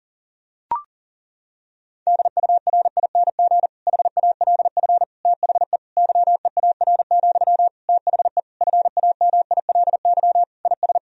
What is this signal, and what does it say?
Morse code, heard as a keyed tone, about 40 wpm, 700 Hz. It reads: DURING HALF THE YEAR, THE FAMILY IS
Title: Sentences from Top 1000 Words - Review Part 2 - 40wpm